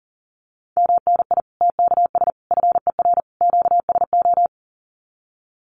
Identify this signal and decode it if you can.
Morse code, heard as a keyed tone, about 37 wpm, 700 Hz. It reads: MNI TKS FER QSO
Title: Common QSO Phrases 40wpm